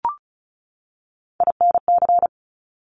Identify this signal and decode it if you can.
Morse code, heard as a keyed tone, about 36 wpm, 700 Hz. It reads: INC
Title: Top Three Letter Combinations in Words 35wpm